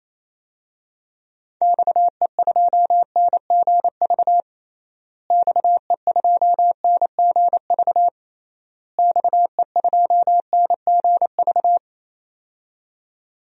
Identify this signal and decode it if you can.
Morse code, heard as a keyed tone, about 28 wpm, 700 Hz. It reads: XE2NGV XE2NGV XE2NGV
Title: DXCC-100 Call Signs - 6 Characters - Repeated 3x - 28wpm